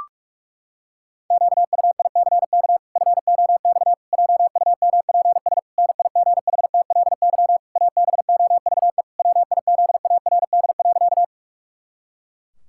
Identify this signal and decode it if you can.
Morse code, heard as a keyed tone, about 46 wpm, 700 Hz. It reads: QUICK FOX JUMPS NIGHTLY ABOVE WIZARD.
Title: Pangram Challenge v2 - 45wpm